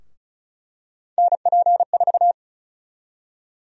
Morse code, heard as a keyed tone, about 34 wpm, 700 Hz. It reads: NP4
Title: USA Call Sign Prefixes - 35wpm